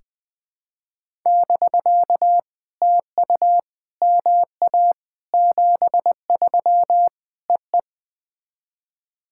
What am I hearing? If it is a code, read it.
Morse code, heard as a keyed tone, about 20 wpm, 700 Hz. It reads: <BK> TU MA 73 EE
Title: POTA QSO Practice - 20wpm